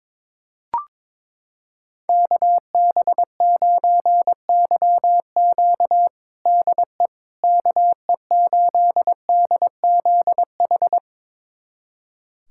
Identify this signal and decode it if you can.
Morse code, heard as a keyed tone, about 22 wpm, 700 Hz. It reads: KB9YQ DE KE8DZH
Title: USA Callsign DE Callsign 22wpm